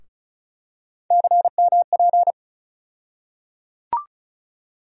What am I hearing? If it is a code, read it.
Morse code, heard as a keyed tone, about 36 wpm, 700 Hz. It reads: CMP
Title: USA Call Sign Postfixes - 3 Letters 35wpm